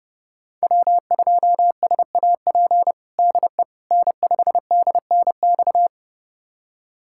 Morse code, heard as a keyed tone, about 32 wpm, 700 Hz. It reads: W2SAP DE N5DNX
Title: USA Callsign DE Callsign - 5 Characters - 30wpm